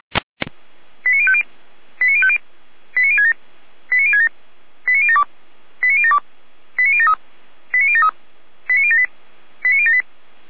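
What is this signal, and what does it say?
Five-tone selective call (ZVEI), heard as quick runs of five tones, tones of about 70 ms. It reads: 89050 89050 89067 89067 89072 89072 89073 89073 89078 89078
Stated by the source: ZVEI five-tone selective call received by SDR.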